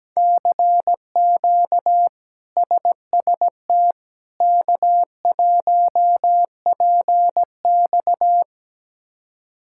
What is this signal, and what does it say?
Morse code, heard as a keyed tone, about 17 wpm, 700 Hz. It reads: CQ SST K1PX
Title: K1USN SST QSO Practice - 17wpm